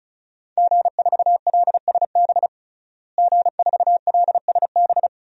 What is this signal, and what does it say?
Morse code, heard as a keyed tone, about 36 wpm, 700 Hz. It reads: G4LSB G4LSB
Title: DXCC-100 Call Signs - 5 Characters - Repeated 3x - 35wpm